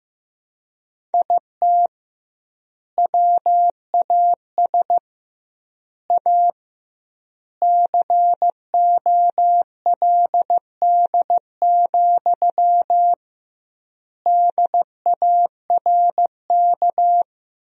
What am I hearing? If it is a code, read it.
Morse code, heard as a keyed tone, about 15 wpm, 700 Hz. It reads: IT WAS A COLD, DARK
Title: 2x Word-Spacing - Sentences from Top 500 Words - 15wpm